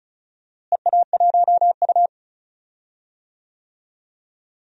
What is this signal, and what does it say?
Morse code, heard as a keyed tone, about 36 wpm, 700 Hz. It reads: EA1U
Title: DXCC-100 Call Signs - 4 Characters - Repeated 2x - 35wpm